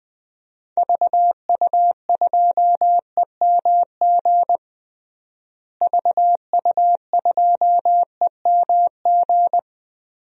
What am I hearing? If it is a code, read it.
Morse code, heard as a keyed tone, about 21 wpm, 700 Hz. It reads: VU2EMG VU2EMG
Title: DXCC-100 Call Signs - 6 Characters - Repeated 2x - 20wpm